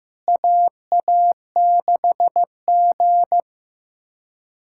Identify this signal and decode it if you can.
Morse code, heard as a keyed tone, about 15 wpm, 700 Hz. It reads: AA6G
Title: USA Callsign DE Callsign - 4 Characters - 15wpm